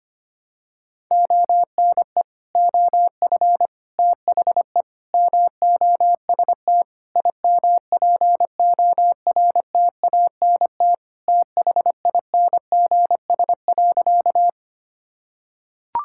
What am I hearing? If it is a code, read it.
Morse code, heard as a keyed tone, about 25 wpm, 700 Hz. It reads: ONE OF THE MOST IMPORTANT THINGS.
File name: practice / 25wpm / Sentences with 4-Word Phrases - 25wpm